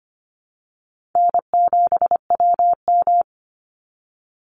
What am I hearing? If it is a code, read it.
Morse code, heard as a keyed tone, about 25 wpm, 700 Hz. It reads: N7WM
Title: NA SOTA Chaser Callsigns 25wpm